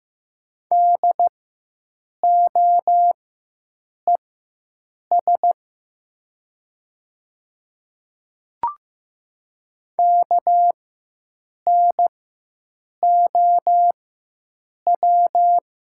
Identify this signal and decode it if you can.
Morse code, heard as a keyed tone, about 15 wpm, spaced as for 7 wpm, 700 Hz. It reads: DOES KNOW
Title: 4x Character Spacing - Top 200 Words - 15wpm